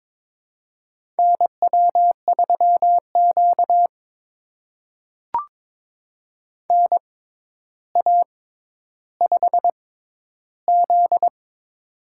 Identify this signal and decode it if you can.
Morse code, heard as a keyed tone, about 23 wpm, spaced as for 11 wpm, 700 Hz. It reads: NW3Q NA5Z
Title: USA Call Signs - 4 Characters - 6x Character Spacing - 22wpm